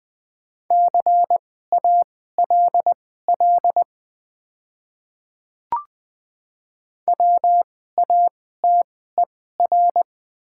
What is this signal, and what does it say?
Morse code, heard as a keyed tone, about 20 wpm, spaced as for 14 wpm, 700 Hz. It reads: CALL WATER
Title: 2x Character Spacing - Top 200 Words - 20wpm